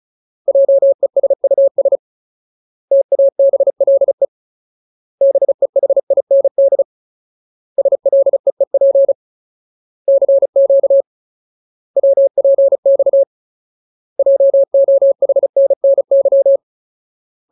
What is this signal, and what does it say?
Morse code, heard as a keyed tone, about 34 wpm, 550 Hz. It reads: JESUS TABLE BEHIND SLEEP CQ WPX JOHNNY
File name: Ultimate Word List / 35WPM Ultimate